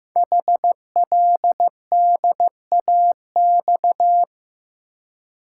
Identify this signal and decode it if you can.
Morse code, heard as a keyed tone, about 15 wpm, 700 Hz. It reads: HLDAX
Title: Code Groups - 5 Letters - 15wpm